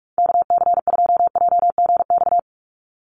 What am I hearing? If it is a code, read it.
Morse code, heard as a keyed tone, about 46 wpm, 700 Hz. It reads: KC2JGX
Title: USA Call Signs 45wpm